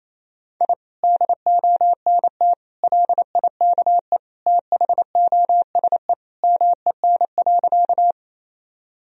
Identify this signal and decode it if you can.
Morse code, heard as a keyed tone, about 28 wpm, 700 Hz. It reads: I DONT LIKE THOSE MEN.
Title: Sentences from Top 1000 Words - Review Part 1 - 28wpm